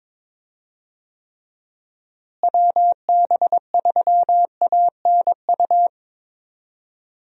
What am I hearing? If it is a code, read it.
Morse code, heard as a keyed tone, about 22 wpm, 700 Hz. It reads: WB3ANU
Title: USA Call Signs - 6 Characters - Repeated 2x - 22wpm